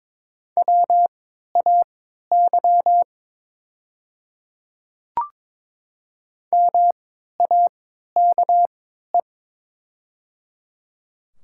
Morse code, heard as a keyed tone, about 22 wpm, spaced as for 12 wpm, 690 Hz. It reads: WAY MAKE
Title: 3x Character Spacing - Top 200 Words - 22wpm